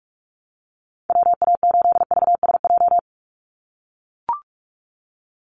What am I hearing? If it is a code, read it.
Morse code, heard as a keyed tone, about 44 wpm, 700 Hz. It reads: WA8VSJ
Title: Call Signs - 45wpm